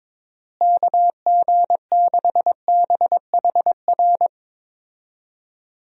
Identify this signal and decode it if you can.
Morse code, heard as a keyed tone, about 22 wpm, 700 Hz. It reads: KG6BHR
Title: USA Call Signs - 6 Characters - Repeated 3x - 22wpm